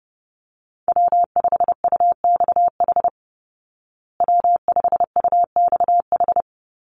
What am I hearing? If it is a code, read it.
Morse code, heard as a keyed tone, about 30 wpm, 700 Hz. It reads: W5UXH W5UXH
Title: Call Signs Repeated 2x - 30wpm